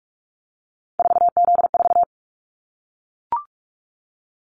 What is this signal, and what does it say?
Morse code, heard as a keyed tone, about 44 wpm, 700 Hz. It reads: 4Z4
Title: Callsign Prefixes for Top 50 DXCC - 45wpm